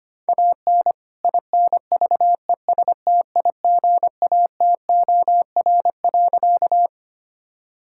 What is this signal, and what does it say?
Morse code, heard as a keyed tone, about 25 wpm, 700 Hz. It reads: AN INVESTIGATOR.
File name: practice / 25wpm / All-Age Appropriate Jokes - 25wpm